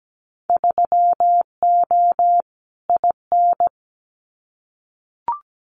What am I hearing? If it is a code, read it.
Morse code, heard as a keyed tone, about 17 wpm, 700 Hz. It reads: 3O IN
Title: Winter Field Day Exchanges Encore 17wpm